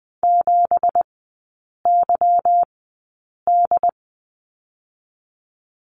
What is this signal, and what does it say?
Morse code, heard as a keyed tone, about 20 wpm, spaced as for 8 wpm, 690 Hz. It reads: 7YD
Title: 198 Lesson 32e - Letter k - Three Characters 20wpm